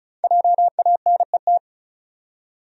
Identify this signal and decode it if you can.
Morse code, heard as a keyed tone, about 36 wpm, 700 Hz. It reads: JANET
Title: USA Common Names - 35wpm